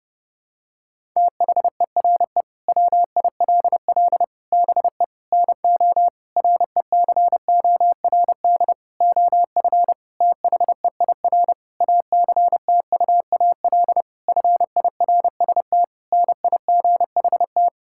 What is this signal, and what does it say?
Morse code, heard as a keyed tone, about 30 wpm, 700 Hz. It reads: THERE WILL BE NO RECORD OF THEIR ACTUAL FIRST NIGHT
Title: Sentences from Top 1000 Words - Review Part 5 - 30wpm